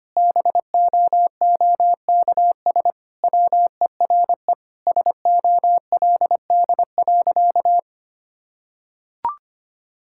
Morse code, heard as a keyed tone, about 25 wpm, 700 Hz. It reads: BOOKS WERE SOLD.
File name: practice / 25wpm / Sentences with 4-Word Phrases - 25wpm